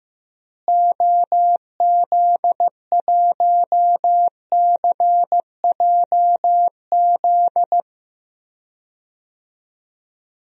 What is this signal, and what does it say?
Morse code, heard as a keyed tone, about 15 wpm, 700 Hz. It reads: OZ1CJZ
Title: DXCC-100 Call Signs - 6 Characters - Repeated 3x - 15wpm